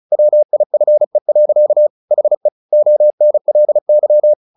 Morse code, heard as a keyed tone, about 34 wpm, 600 Hz. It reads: WIFE. HE ONLY
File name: the old man and the sea / the old man and the sea 1 35wpm